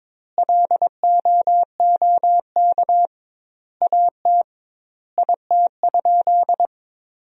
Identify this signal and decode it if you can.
Morse code, heard as a keyed tone, about 23 wpm, 700 Hz. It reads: LOOK AT IT?
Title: 2x Word-Spacing - Sentences from Top 100 Words - 22wpm